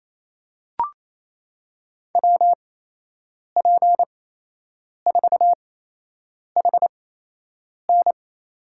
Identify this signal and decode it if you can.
Morse code, heard as a keyed tone, about 28 wpm, spaced as for 8 wpm, 700 Hz. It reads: WP4HN
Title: USA Call Signs - 5 Characters - 8x Character Spacing - 28wpm